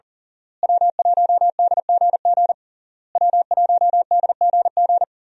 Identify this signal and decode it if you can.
Morse code, heard as a keyed tone, about 40 wpm, 700 Hz. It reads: W1DGG W1DGG
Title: USA Call Signs - 5 Characters - Repeated 3x - 40wpm